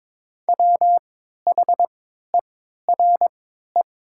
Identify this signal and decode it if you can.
Morse code, heard as a keyed tone, about 22 wpm, spaced as for 12 wpm, 700 Hz. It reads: WHERE
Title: 3x Character Spacing - Top 300 Words - 22wpm